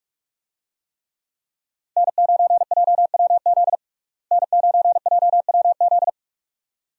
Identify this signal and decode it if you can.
Morse code, heard as a keyed tone, about 42 wpm, 700 Hz. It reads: N9JWZ N9JWZ
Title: USA Call Signs - 5 Characters - Repeated 2x - 45wpm